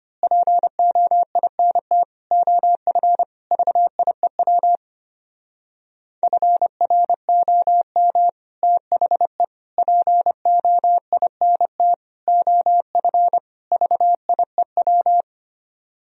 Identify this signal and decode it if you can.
Morse code, heard as a keyed tone, about 26 wpm, 700 Hz. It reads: POINT OF VIEW FROM THE POINT OF VIEW
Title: Speed-Racing - Sets of 5 Words 25wpm